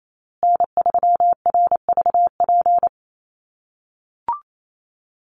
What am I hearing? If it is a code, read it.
Morse code, heard as a keyed tone, about 28 wpm, 700 Hz. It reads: N3RVP
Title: USA Call Signs Repeated 2x 28wpm